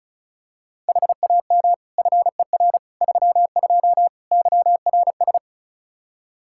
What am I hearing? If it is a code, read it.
Morse code, heard as a keyed tone, about 36 wpm, 700 Hz. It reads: HAM FER 32 YRS
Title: Practice QSO - Exchange 3 - More Info - 35wpm